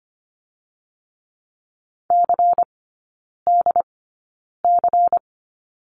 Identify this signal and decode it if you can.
Morse code, heard as a keyed tone, about 25 wpm, spaced as for 9 wpm, 700 Hz. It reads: CDC
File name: course / 25:10wpm / 093 Lesson 16e - Letter c - Three Characters 25wpm